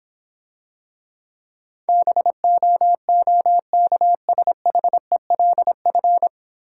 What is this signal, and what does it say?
Morse code, heard as a keyed tone, about 25 wpm, 700 Hz. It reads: BOOKSHELF
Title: Easy Words - Speed-Racing with Hints - 20wpm